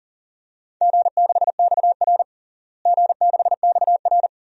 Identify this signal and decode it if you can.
Morse code, heard as a keyed tone, about 40 wpm, 700 Hz. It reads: G6XR G6XR
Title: DXCC-100 Call Signs - 4 Characters - Repeated 2x - 40wpm